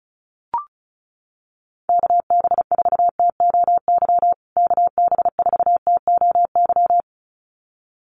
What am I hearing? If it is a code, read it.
Morse code, heard as a keyed tone, about 34 wpm, 700 Hz. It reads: KB4TOY KB4TOY
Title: USA Call Signs Repeated 2x 35wpm